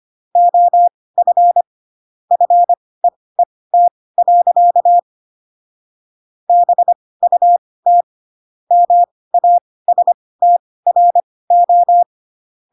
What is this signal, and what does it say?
Morse code, heard as a keyed tone, about 25 wpm, spaced as for 18 wpm, 700 Hz. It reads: OF FEET. BUT MASTRO